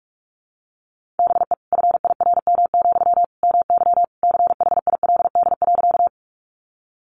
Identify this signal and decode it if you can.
Morse code, heard as a keyed tone, about 44 wpm, 700 Hz. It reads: BE FIRM, MY CHILD.
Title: Sentences from Top 1400 Words 45wpm